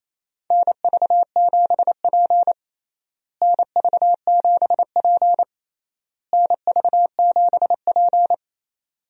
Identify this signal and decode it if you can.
Morse code, heard as a keyed tone, about 28 wpm, 700 Hz. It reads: NV7P NV7P NV7P
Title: USA Call Signs - 4 Characters - Repeated 3x - 28wpm